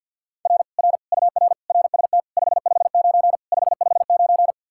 Morse code, heard as a keyed tone, about 60 wpm, 700 Hz. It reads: R R UR RST 559 559
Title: Speed-Racing Common QSO Phrases - 50wpm